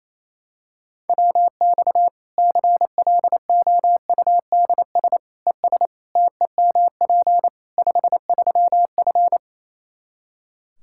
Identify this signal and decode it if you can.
Morse code, heard as a keyed tone, about 27 wpm, 700 Hz. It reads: WX CLOUDS ES TEMP 53F
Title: Practice QSO - Exchange 2 - Info - 28wpm